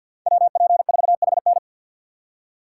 Morse code, heard as a keyed tone, about 48 wpm, 700 Hz. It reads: WP4HN
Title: USA Call Signs - 5 Characters - 4x Character Spacing - 50wpm